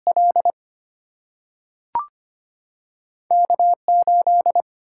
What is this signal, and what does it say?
Morse code, heard as a keyed tone, about 25 wpm, 700 Hz. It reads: L K8
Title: Single-Letter Single-Number 25wpm